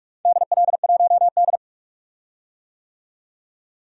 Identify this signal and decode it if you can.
Morse code, heard as a keyed tone, about 46 wpm, 700 Hz. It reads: DL1D
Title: DXCC-100 Call Signs - 4 Characters - 45wpm